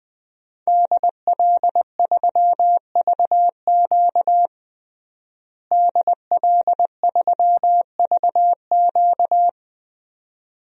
Morse code, heard as a keyed tone, about 21 wpm, 700 Hz. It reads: DL3VQ DL3VQ
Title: DXCC-100 Call Signs - 5 Characters - Repeated 2x - 20wpm